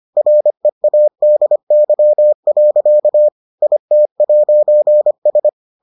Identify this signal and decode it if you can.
Morse code, heard as a keyed tone, about 25 wpm, 600 Hz. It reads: READY. IT'S